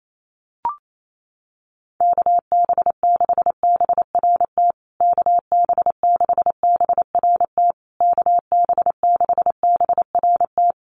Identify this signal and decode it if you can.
Morse code, heard as a keyed tone, about 28 wpm, 700 Hz. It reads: KB6BRT KB6BRT KB6BRT
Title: USA Call Signs Repeated 3x 28wpm